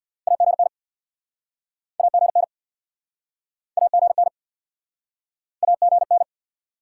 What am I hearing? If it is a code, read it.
Morse code, heard as a keyed tone, about 66 wpm, 700 Hz. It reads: AGN AGN AGN AGN